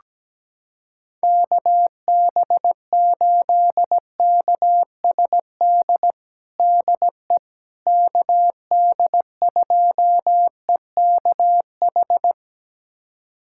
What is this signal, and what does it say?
Morse code, heard as a keyed tone, about 17 wpm, 700 Hz. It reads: KB8KSD DE KD2EKH
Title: USA Callsign DE Callsign 17wpm